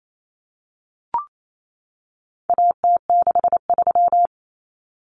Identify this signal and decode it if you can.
Morse code, heard as a keyed tone, about 28 wpm, 700 Hz. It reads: AT63
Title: Common Cut Numbers - 4-Digit 28wpm